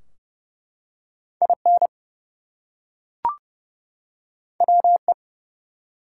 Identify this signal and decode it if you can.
Morse code, heard as a keyed tone, about 29 wpm, 700 Hz. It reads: IN WE
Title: Two Letter Words - 30wpm